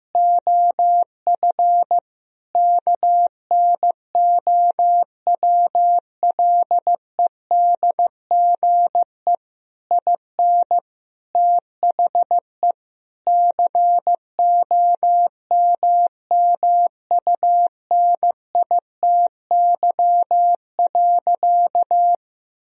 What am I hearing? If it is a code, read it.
Morse code, heard as a keyed tone, about 15 wpm, 700 Hz. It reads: OF KNOWLEDGE IN THE COMMUNITY.
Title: Sentences from Top 1000 Words - Review Part 4 - 15wpm